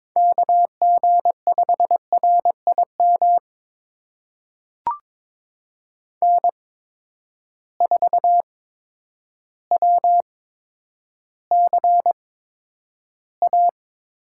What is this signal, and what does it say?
Morse code, heard as a keyed tone, about 22 wpm, spaced as for 10 wpm, 700 Hz. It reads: KG5RIM N4WCA
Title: USA Call Signs - 8x Character Spacing - 22wpm